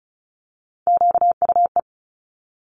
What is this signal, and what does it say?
Morse code, heard as a keyed tone, about 33 wpm, 700 Hz. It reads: QUE
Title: Multi-letter Phonemes Quick Fire 35wpm